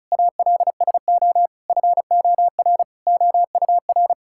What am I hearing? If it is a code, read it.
Morse code, heard as a keyed tone, about 36 wpm, 700 Hz. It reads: ALSO FOR OUR
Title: Sentences from Top 1000 Words - Review Part 6 - 35wpm